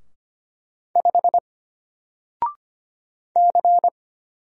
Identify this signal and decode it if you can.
Morse code, heard as a keyed tone, about 25 wpm, 700 Hz. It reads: 5 C
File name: Single-Letter-Number-25wpm